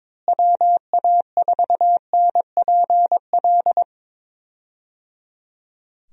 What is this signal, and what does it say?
Morse code, heard as a keyed tone, about 21 wpm, 700 Hz. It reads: WA4NPL
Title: USA Call Signs - 6 Characters - Repeated 3x - 22wpm